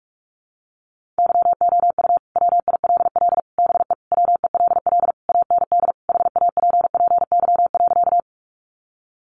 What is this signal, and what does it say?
Morse code, heard as a keyed tone, about 44 wpm, 700 Hz. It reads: YOU WILL BE WELL AND HAPPY.
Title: Sentences from Top 1000 Words 45wpm